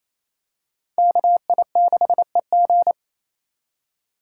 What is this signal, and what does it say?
Morse code, heard as a keyed tone, about 28 wpm, 700 Hz. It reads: KI6EG
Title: USA Call Signs - 5 Characters - 4x Character Spacing - 28wpm